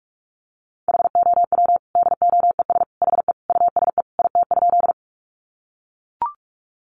Morse code, heard as a keyed tone, about 44 wpm, 700 Hz. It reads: HOW DOES HE USE IT?
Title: Sentences from Top 200 Words 45wpm